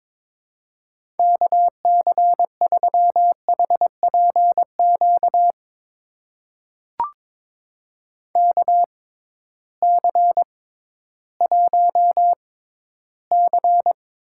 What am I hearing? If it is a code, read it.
Morse code, heard as a keyed tone, about 22 wpm, spaced as for 13 wpm, 700 Hz. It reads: KC3HPQ KC1C
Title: USA Call Signs - 6 Characters - 6x Character Spacing - 22wpm